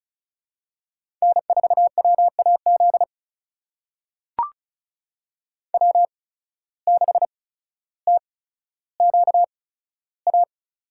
Morse code, heard as a keyed tone, about 36 wpm, spaced as for 15 wpm, 700 Hz. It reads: N4WAZ W6TQA
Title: USA Call Signs - 5 Characters - 8x Character Spacing - 35wpm